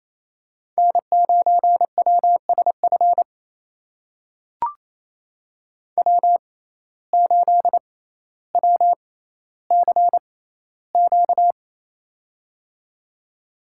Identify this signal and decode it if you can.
Morse code, heard as a keyed tone, about 28 wpm, spaced as for 14 wpm, 700 Hz. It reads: N9WSF W8WCQ
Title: USA Call Signs - 6x Character Spacing - 28wpm